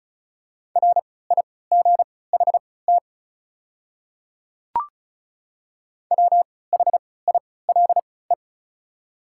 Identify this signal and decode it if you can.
Morse code, heard as a keyed tone, about 34 wpm, spaced as for 20 wpm, 700 Hz. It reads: RIGHT WHILE
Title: 3x Character Spacing - Top 300 Words - 35wpm